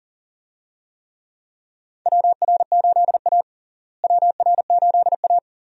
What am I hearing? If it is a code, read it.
Morse code, heard as a keyed tone, about 37 wpm, 700 Hz. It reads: WR8A WR8A
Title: USA Call Signs - 4 Characters - Repeated 2x - 40wpm